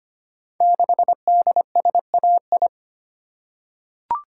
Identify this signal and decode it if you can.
Morse code, heard as a keyed tone, about 25 wpm, 700 Hz. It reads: 6DSAI
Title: Code Groups - 5 Letter:Numbers - 25wpm